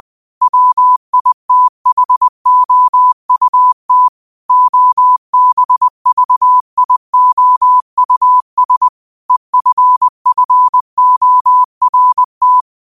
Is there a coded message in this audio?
Yes: Morse code from 0.4 to 12.6 s, heard as a keyed tone, about 20 wpm, 1000 Hz. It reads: WITHOUT OBVIOUS EFFORT